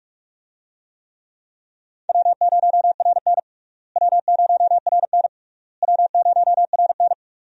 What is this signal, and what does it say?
Morse code, heard as a keyed tone, about 46 wpm, 700 Hz. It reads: W0RN W0RN W0RN
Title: USA Call Signs - 4 Characters - Repeated 3x - 45wpm